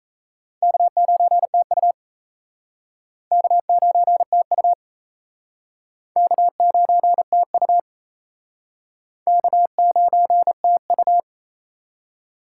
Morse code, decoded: K9TU K9TU K9TU K9TU

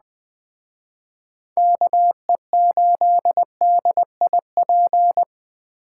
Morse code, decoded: KE8DIP

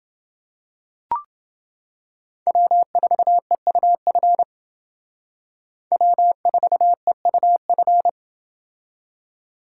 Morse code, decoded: W4EUF W4EUF

